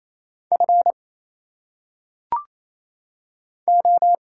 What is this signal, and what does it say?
Morse code, heard as a keyed tone, about 27 wpm, 700 Hz. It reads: F O
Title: Single Letters - Phonetic Pronunciation 28wpm